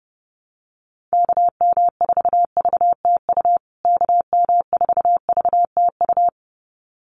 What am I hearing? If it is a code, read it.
Morse code, heard as a keyed tone, about 28 wpm, 700 Hz. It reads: KM4VTU KM4VTU
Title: USA Call Signs Repeated 2x 30wpm